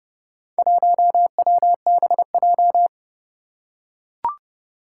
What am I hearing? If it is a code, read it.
Morse code, heard as a keyed tone, about 30 wpm, 700 Hz. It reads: 1WBJ
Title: Four Challenging Letter-Number Combinations 30wpm